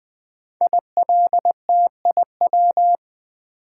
Morse code, heard as a keyed tone, about 20 wpm, 700 Hz. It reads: ILTIW
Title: Code Groups - 5 Letters - 20wpm